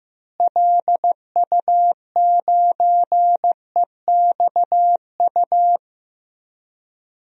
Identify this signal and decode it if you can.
Morse code, heard as a keyed tone, about 15 wpm, 700 Hz. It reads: LU9EXU
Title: DXCC-100 Call Signs - 6 Characters - Repeated 3x - 15wpm